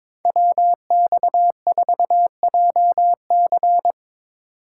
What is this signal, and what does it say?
Morse code, heard as a keyed tone, about 22 wpm, 700 Hz. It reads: WX4JC
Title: USA Call Signs - 5 Characters - 4x Character Spacing - 22wpm